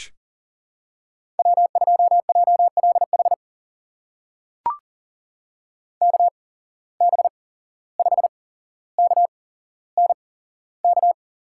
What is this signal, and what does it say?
Morse code, heard as a keyed tone, about 42 wpm, spaced as for 16 wpm, 700 Hz. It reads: W2JLH KB5KNK
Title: USA Call Signs - 8x Character Spacing - 40wpm